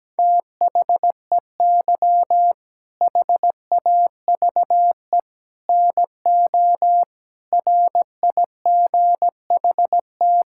Morse code, heard as a keyed tone, about 17 wpm, 700 Hz. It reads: THEY HAVE NO RIGHT
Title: Sentences from Top 1000 Words - Review Part 6 - 17wpm